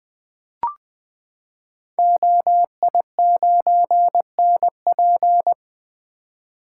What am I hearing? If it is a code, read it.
Morse code, heard as a keyed tone, about 20 wpm, 700 Hz. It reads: OI9NP
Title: Code Groups - 5 Letter:Numbers - 20wpm